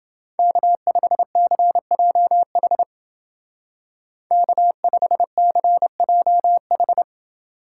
Morse code, heard as a keyed tone, about 31 wpm, 700 Hz. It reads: K5CJH K5CJH